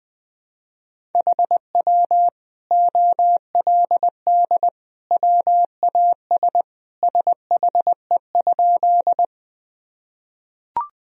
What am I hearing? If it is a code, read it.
Morse code, heard as a keyed tone, about 20 wpm, 700 Hz. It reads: HW OLD WAS SHE?
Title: Sentences from Top 200 Words and Abbreviations 20wpm